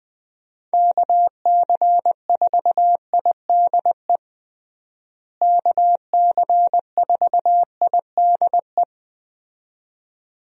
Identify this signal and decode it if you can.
Morse code, heard as a keyed tone, about 20 wpm, 700 Hz. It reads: KC4IDE KC4IDE
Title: USA Call Signs - 6 Characters - Repeated 2x - 20wpm